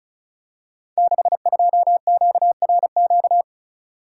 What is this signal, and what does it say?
Morse code, heard as a keyed tone, about 36 wpm, 700 Hz. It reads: 62QRQ
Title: Code Groups - 5 Letter:Numbers - 35wpm